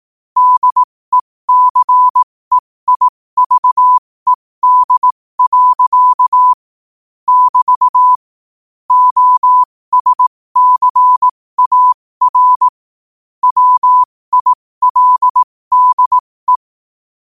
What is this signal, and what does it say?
Morse code, heard as a keyed tone, about 18 wpm, spaced as for 15 wpm, 1000 Hz. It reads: DECEIVED. <BT> OSCAR WILDE
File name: qotdfiles / 15wpm